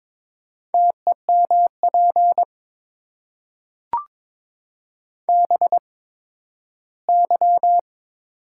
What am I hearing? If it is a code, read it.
Morse code, heard as a keyed tone, about 22 wpm, spaced as for 13 wpm, 690 Hz. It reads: TEMP BY
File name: Top 90 QSO Elements - 8x Character Spacing - 22wpm